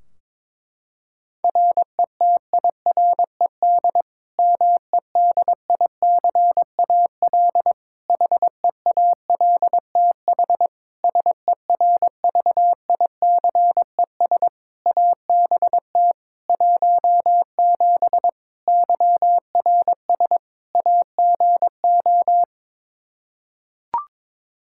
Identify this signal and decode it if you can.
Morse code, heard as a keyed tone, about 22 wpm, 700 Hz. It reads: RETIRED MEDICAL HEALTH SERVICES ABT 17 YRS AGO
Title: K1USN SST QSO Practice 22wpm